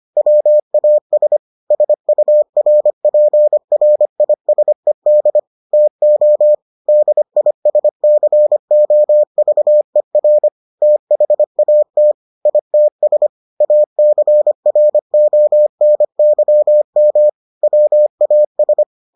Morse code, heard as a keyed tone, about 25 wpm, 600 Hz. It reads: WAS SURPRISED TO DISCOVER THAT ITS ACRONYM WAS